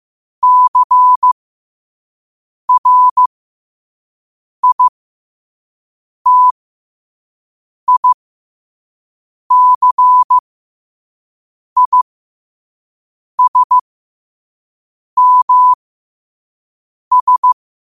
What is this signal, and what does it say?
Morse code, heard as a keyed tone, about 15 wpm, spaced as for 5 wpm, 1000 Hz. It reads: CRITICISMS
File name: qotdfiles / 5wpm